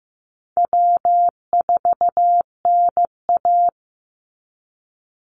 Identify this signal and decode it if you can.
Morse code, heard as a keyed tone, about 15 wpm, 700 Hz. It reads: W4NA
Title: Easy Call Signs - 15wpm